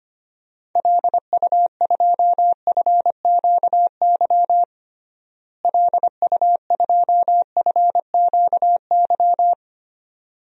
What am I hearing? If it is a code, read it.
Morse code, heard as a keyed tone, about 25 wpm, 700 Hz. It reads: LU2FQY LU2FQY
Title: DXCC-100 Call Signs - 6 Characters - Repeated 2x - 25wpm